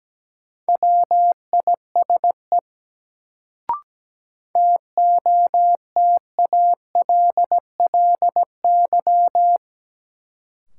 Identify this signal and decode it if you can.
Morse code, heard as a keyed tone, about 17 wpm, 700 Hz. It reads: WISE TOTALLY
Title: Sentences from Top 1600 Words - New Word Drill Alt - 17wpm